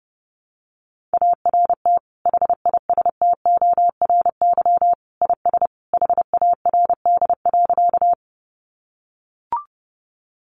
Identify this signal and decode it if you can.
Morse code, heard as a keyed tone, about 30 wpm, 700 Hz. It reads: ART HISTORY IS HARD.
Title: Sentences from Top 600 Words 30wpm